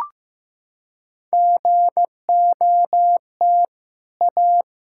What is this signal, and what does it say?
Morse code, heard as a keyed tone, about 15 wpm, 700 Hz. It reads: GOT A
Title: Sentences from Top 1600 Words 15wpm